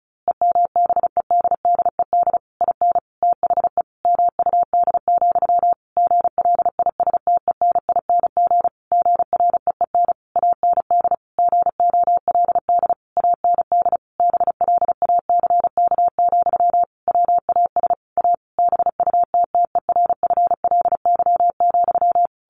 Morse code, heard as a keyed tone, about 34 wpm, 700 Hz. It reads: EMBEDDED IN THE MUD, GLISTENING GREEN AND GOLD AND BLACK, WAS A BUTTERFLY,
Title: A Sound Of Thunder 35wpm